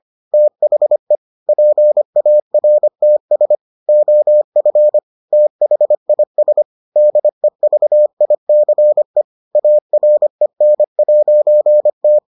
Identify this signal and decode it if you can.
Morse code, heard as a keyed tone, about 25 wpm, 600 Hz. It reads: THE PARTS OF THIS DEVICE AREN'T